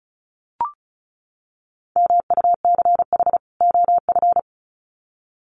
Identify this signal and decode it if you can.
Morse code, heard as a keyed tone, about 34 wpm, 700 Hz. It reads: MUCH OF
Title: Sets of 2 Words 35wpm